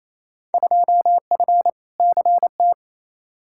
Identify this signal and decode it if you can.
Morse code, heard as a keyed tone, about 28 wpm, 700 Hz. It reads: 2F CT
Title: Field Day Exchanges Encore - 28wpm